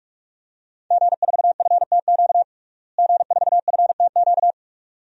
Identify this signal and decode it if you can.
Morse code, heard as a keyed tone, about 46 wpm, 700 Hz. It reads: G4FTQ G4FTQ
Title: DXCC-100 Call Signs - 5 Characters - Repeated 2x - 45wpm